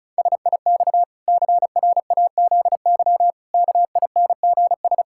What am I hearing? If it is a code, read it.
Morse code, heard as a keyed tone, about 34 wpm, 700 Hz. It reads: SIX CRAZY KINGS
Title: Pangram Challenge v2 - 35wpm